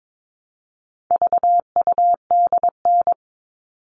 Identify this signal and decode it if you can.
Morse code, heard as a keyed tone, about 22 wpm, 700 Hz. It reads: VUDN
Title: All Cut Numbers - 4-Digit 22wpm